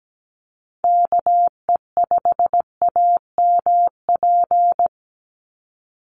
Morse code, heard as a keyed tone, about 17 wpm, 700 Hz. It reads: KE5AMP
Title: USA Call Signs 17wpm